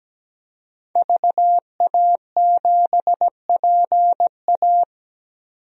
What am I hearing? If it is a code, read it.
Morse code, heard as a keyed tone, about 17 wpm, 700 Hz. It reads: VA7PA
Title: DXCC-100 Call Signs - 5 Characters - 17wpm